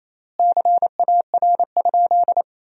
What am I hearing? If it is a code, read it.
Morse code, heard as a keyed tone, about 27 wpm, 700 Hz. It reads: CAR?
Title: 4x Word-Spacing - Sentences from Top 300 Words - 28wpm